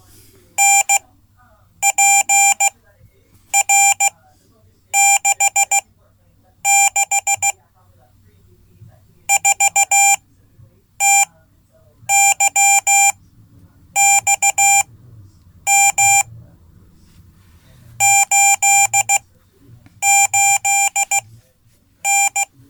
Morse code, heard as a keyed tone, about 16 wpm, spaced as for 8 wpm, 790 Hz. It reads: NPR66 4TYXM 88N
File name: Code / 5min-c